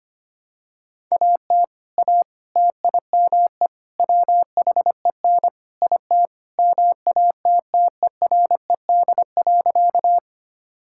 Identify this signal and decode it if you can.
Morse code, heard as a keyed tone, about 25 wpm, 700 Hz. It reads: AT A TIME WHEN IT MATTERED.
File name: practice / 25wpm / Sentences with 4-Word Phrases - 25wpm